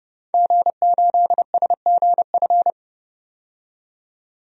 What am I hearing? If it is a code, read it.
Morse code, heard as a keyed tone, about 30 wpm, 690 Hz. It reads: G8SGF
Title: DXCC-100 Call Signs - 5 Characters - 30wpm